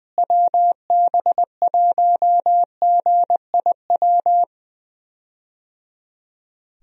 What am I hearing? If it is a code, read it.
Morse code, heard as a keyed tone, about 20 wpm, 700 Hz. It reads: WB1GIW
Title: USA Call Signs - 6 Characters - Repeated 3x - 20wpm